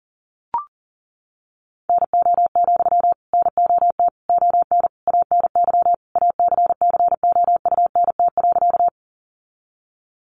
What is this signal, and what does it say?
Morse code, heard as a keyed tone, about 40 wpm, 700 Hz. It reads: NO, NOT ON ANY ACCOUNT.
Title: Sentences from Top 900 Words 40wpm